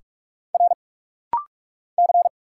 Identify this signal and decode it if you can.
Morse code, heard as a keyed tone, about 45 wpm, 700 Hz. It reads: R C
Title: Single Letter-Number Alt - Rapid-Fire 45wpm